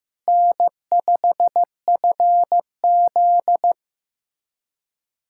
Morse code, heard as a keyed tone, about 15 wpm, 700 Hz. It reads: N5FZ
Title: USA Callsign DE Callsign - 4 Characters - 15wpm